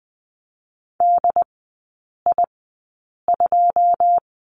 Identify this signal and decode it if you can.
Morse code, heard as a keyed tone, about 20 wpm, spaced as for 8 wpm, 700 Hz. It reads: DI2
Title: 081 Lesson 14e - Number 2 - Three Characters 20wpm